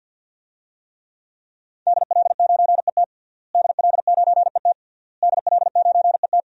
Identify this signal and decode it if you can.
Morse code, heard as a keyed tone, about 52 wpm, 700 Hz. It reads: DL9ET DL9ET DL9ET
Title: DXCC-100 Call Signs - 5 Characters - Repeated 3x - 50wpm